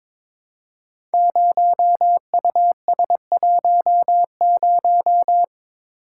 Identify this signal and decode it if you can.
Morse code, heard as a keyed tone, about 22 wpm, 700 Hz. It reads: 0US10
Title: Code Groups - 5 Letter:Numbers - 22wpm